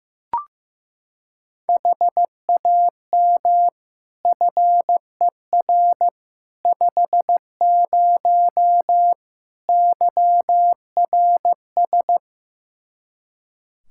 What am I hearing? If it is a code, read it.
Morse code, heard as a keyed tone, about 15 wpm, 700 Hz. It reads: HAM FER 50 YRS
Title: Practice QSO - Exchange 3 - More Info - 15wpm